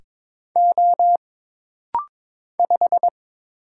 Morse code, heard as a keyed tone, about 22 wpm, 700 Hz. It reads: O 5
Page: This rapid-fire timing is geared to push your ICR (Instant Character Recognition) skills.